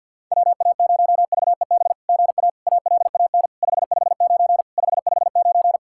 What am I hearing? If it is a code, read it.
Morse code, heard as a keyed tone, about 50 wpm, 690 Hz. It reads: WA0VEB GA ALAN 559 559
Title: SOTA QSO Practice 50wpm